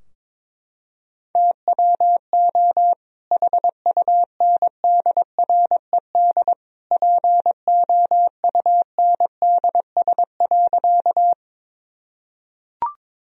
Morse code, heard as a keyed tone, about 22 wpm, 700 Hz. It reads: TWO HUNDRED POUNDS.
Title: Sentences from Top 1000 Words - Review Part 2 - 22wpm